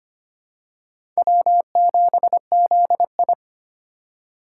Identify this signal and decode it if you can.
Morse code, heard as a keyed tone, about 25 wpm, 700 Hz. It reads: W7ZI
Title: USA Call Signs - 4 Characters - 2x Character Spacing - 25wpm